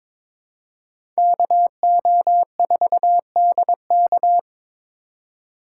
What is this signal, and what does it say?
Morse code, heard as a keyed tone, about 22 wpm, 700 Hz. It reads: KO4DK
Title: USA Call Signs - 5 Characters - 4x Character Spacing - 22wpm